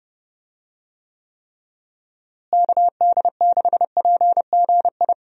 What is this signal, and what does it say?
Morse code, heard as a keyed tone, about 32 wpm, 700 Hz. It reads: KD6PGI
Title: USA Call Signs - 6 Characters - Repeated 3x - 30wpm